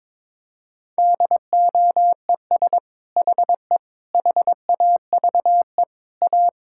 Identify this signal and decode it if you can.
Morse code, heard as a keyed tone, about 22 wpm, 700 Hz. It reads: DOES HE HAVE A
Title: Sentences from Top 1000 Words - Review Part 5 - 22wpm